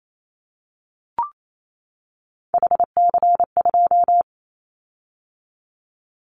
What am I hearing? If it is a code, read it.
Morse code, heard as a keyed tone, about 28 wpm, 700 Hz. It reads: HC2
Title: Callsign Prefixes for Top 100 DXCC - 28wpm